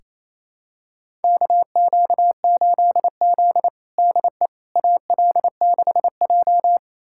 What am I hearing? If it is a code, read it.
Morse code, heard as a keyed tone, about 28 wpm, 700 Hz. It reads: KQ8Z DE AL6J
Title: USA Callsign DE Callsign - 4 Characters - 28wpm